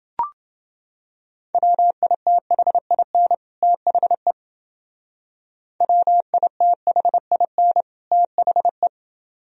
Morse code, heard as a keyed tone, about 30 wpm, 700 Hz. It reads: WITHIN THE WITHIN THE